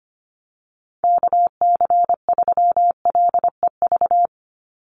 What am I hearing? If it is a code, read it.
Morse code, heard as a keyed tone, about 25 wpm, 690 Hz. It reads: KC3LEV
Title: USA Call Signs 25wpm